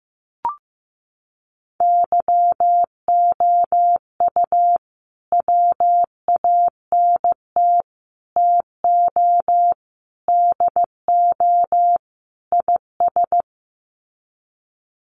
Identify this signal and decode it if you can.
Morse code, heard as a keyed tone, about 15 wpm, 700 Hz. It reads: YOU WANT TO DO IS
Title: Sets of 5 Words Encore2 15wpm